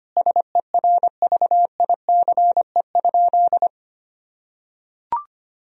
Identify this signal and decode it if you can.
Morse code, heard as a keyed tone, about 25 wpm, 700 Hz. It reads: SERVICE?
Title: Sentences from Top 1000 Words - Review Part 4 - 25wpm